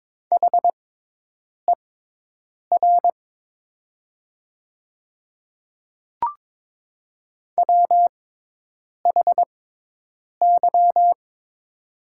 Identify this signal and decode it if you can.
Morse code, heard as a keyed tone, about 22 wpm, spaced as for 8 wpm, 700 Hz. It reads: HER WHY
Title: 6x Character Spacing - Top 500 Words - 22wpm